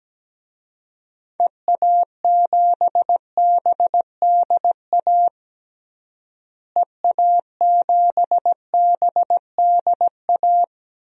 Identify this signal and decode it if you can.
Morse code, heard as a keyed tone, about 17 wpm, 700 Hz. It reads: EA7BDA EA7BDA
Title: DXCC-100 Call Signs - 6 Characters - Repeated 2x - 17wpm